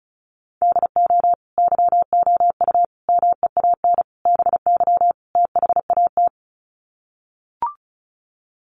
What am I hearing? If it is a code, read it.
Morse code, heard as a keyed tone, about 34 wpm, 700 Hz. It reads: DO YOU MEAN BY THAT
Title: Sets of 5 Words Encore2 35wpm